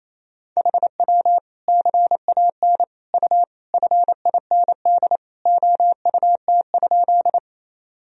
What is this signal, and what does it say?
Morse code, heard as a keyed tone, about 28 wpm, 700 Hz. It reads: HW CAN U FIND OUT?
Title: Sentences from Top 200 Words and Abbreviations 28wpm